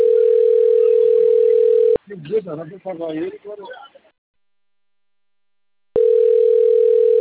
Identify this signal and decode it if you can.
Ringback tone (North America), 440 + 480 Hz, with speech.
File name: out- - - - - .7465